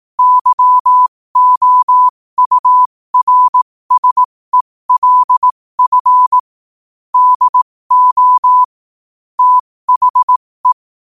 Morse code, heard as a keyed tone, about 18 wpm, spaced as for 15 wpm, 1000 Hz. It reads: YOURSELF DO THE